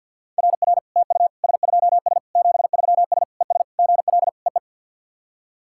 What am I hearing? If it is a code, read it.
Morse code, heard as a keyed tone, about 47 wpm, 690 Hz. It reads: RR TU S2S 73S ES GL EE
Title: SOTA QSO Practice 50wpm